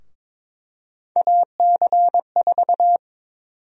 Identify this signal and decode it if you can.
Morse code, heard as a keyed tone, about 22 wpm, 700 Hz. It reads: AC4
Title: USA Call Sign Prefixes - 22wpm